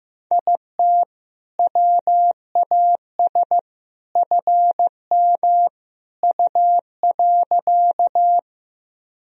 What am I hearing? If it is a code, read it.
Morse code, heard as a keyed tone, about 15 wpm, 700 Hz. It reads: IT WAS FM U.
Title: Sentences from Top 100 Words and Abbreviations 15wpm